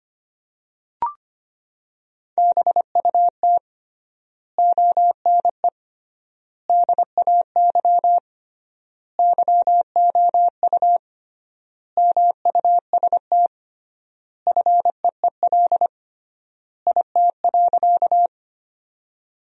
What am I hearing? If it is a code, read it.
Morse code, heard as a keyed tone, about 25 wpm, 700 Hz. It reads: BUT ONE DAY YOU MUST FEEL IT.
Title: 3x Word-Spacing - Sentences from Top 400 Words - 25wpm